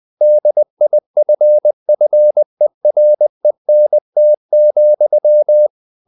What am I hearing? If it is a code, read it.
Morse code, heard as a keyed tone, about 20 wpm, 600 Hz. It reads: DIFFERENT,
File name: the old man and the sea / the old man and the sea part 1 20wpm